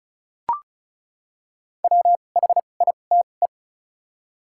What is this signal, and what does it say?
Morse code, heard as a keyed tone, about 36 wpm, spaced as for 25 wpm, 700 Hz. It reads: WHITE
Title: 2x Character Spacing - Top 300 Words - 35wpm